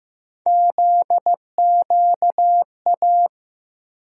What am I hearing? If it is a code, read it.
Morse code, heard as a keyed tone, about 15 wpm, 700 Hz. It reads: ZQA
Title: USA Call Sign Postfixes - 3 Letters 15wpm